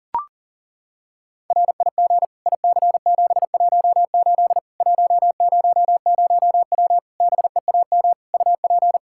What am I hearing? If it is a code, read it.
Morse code, heard as a keyed tone, about 40 wpm, 700 Hz. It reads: RIG IC718 100W BEAM UP
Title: K1USN SST QSO Practice 40wpm